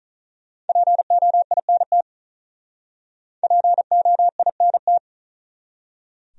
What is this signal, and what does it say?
Morse code, heard as a keyed tone, about 46 wpm, 700 Hz. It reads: POINT POINT